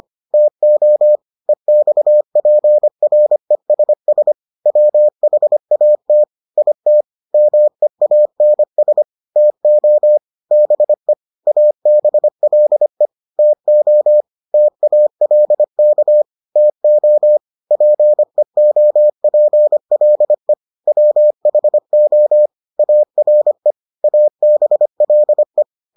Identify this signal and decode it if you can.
Morse code, heard as a keyed tone, about 25 wpm, 600 Hz. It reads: TO EXPRESS WHAT IT MEANS TO BE ABLE TO TALK TO PEOPLE WHO ARE ABLE